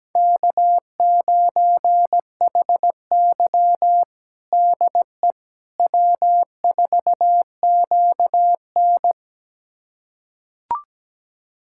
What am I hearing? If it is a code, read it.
Morse code, heard as a keyed tone, about 17 wpm, 700 Hz. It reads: K9HY DE W4QN
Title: USA Callsign DE Callsign - 4 Characters - 17wpm